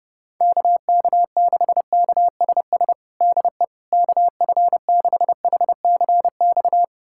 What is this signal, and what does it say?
Morse code, heard as a keyed tone, about 30 wpm, 700 Hz. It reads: KK6KSS DE KF6HCX
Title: USA Callsign DE Callsign - 6 Characters - 30wpm